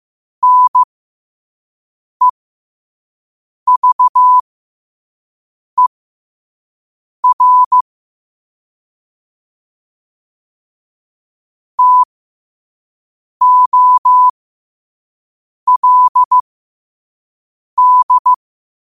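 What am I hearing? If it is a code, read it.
Morse code, heard as a keyed tone, about 15 wpm, spaced as for 5 wpm, 1000 Hz. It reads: NEVER TOLD
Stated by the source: 5 WPM Morse quote of the day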